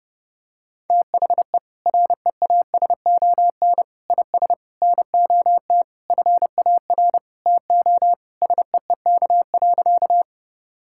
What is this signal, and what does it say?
Morse code, heard as a keyed tone, about 28 wpm, 700 Hz. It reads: THE REASON IS NOT FAR TO SEEK.
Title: Sentences from Top 1600 Words 30wpm